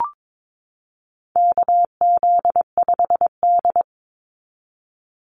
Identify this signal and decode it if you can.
Morse code, heard as a keyed tone, about 22 wpm, 700 Hz. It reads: KZ5D
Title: Call Signs Repeated 2x - 22wpm